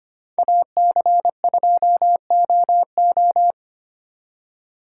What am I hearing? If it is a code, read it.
Morse code, heard as a keyed tone, about 31 wpm, 700 Hz. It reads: AC2OO
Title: Speed Racing - USA 2x2 Call Signs 25wpm